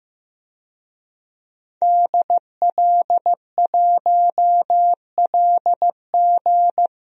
Morse code, heard as a keyed tone, about 15 wpm, 700 Hz. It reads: DL1LG
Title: DXCC-100 Call Signs - 5 Characters - 15wpm